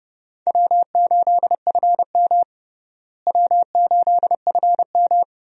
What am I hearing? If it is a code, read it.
Morse code, heard as a keyed tone, about 30 wpm, 700 Hz. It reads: W8FM W8FM
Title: USA Call Signs - 4 Characters - Repeated 2x - 30wpm